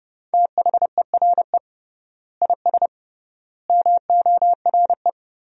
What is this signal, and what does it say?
Morse code, heard as a keyed tone, about 32 wpm, 700 Hz. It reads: THERE IS MORE
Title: 3x Word-Spacing - Sentences from Top 200 Words - 30wpm